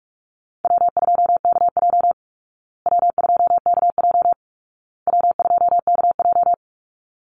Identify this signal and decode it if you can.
Morse code, heard as a keyed tone, about 44 wpm, 700 Hz. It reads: W2KJ W2KJ W2KJ
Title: Easy Call Signs Repeated 3x - 45wpm